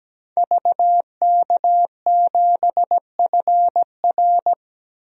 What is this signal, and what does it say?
Morse code, heard as a keyed tone, about 17 wpm, 700 Hz. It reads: VK7FR
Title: DXCC-100 Call Signs - 5 Characters - 17wpm